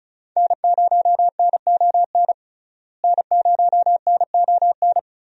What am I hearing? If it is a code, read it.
Morse code, heard as a keyed tone, about 36 wpm, 700 Hz. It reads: N0NON N0NON
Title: USA Call Signs - 5 Characters - Repeated 3x - 35wpm